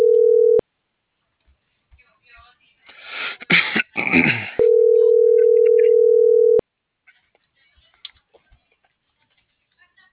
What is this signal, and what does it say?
Ringback tone (North America), 440 + 480 Hz.